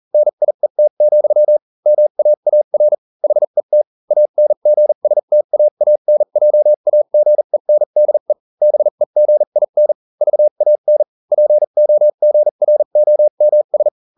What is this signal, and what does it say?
Morse code, heard as a keyed tone, about 40 wpm, spaced as for 35 wpm, 600 Hz. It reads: NIET, MAAR HET ANGSTAANJAGENDE BEGIN VAN POGROMS